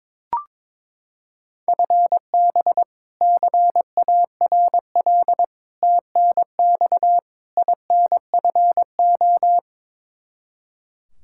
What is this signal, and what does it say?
Morse code, heard as a keyed tone, about 21 wpm, 700 Hz. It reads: FB CARL TNX INFO
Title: Practice QSO - Exchange 3 - More Info - 22wpm